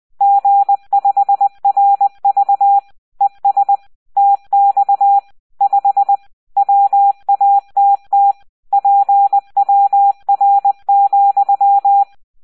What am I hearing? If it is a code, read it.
Morse code, heard as a keyed tone, about 20 wpm, 800 Hz, with radio noise. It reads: G5RV ES TX 5 WATT PWR,